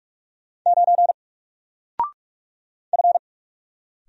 Morse code, heard as a keyed tone, about 44 wpm, 700 Hz. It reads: 9 F
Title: Single Letter-Number Alt 45wpm